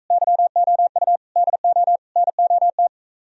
Morse code, heard as a keyed tone, about 45 wpm, 700 Hz. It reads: YOU DO NOT